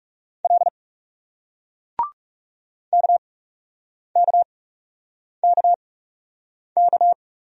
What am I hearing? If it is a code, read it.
Morse code, heard as a keyed tone, about 38 wpm, spaced as for 8 wpm, 700 Hz. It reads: L KKKK